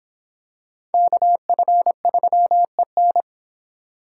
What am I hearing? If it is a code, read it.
Morse code, heard as a keyed tone, about 26 wpm, 700 Hz. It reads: KF3EN